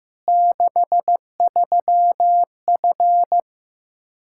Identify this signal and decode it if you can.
Morse code, heard as a keyed tone, about 16 wpm, 700 Hz. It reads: 63F
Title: Practice QSO - Exchange 2 - Info - 15wpm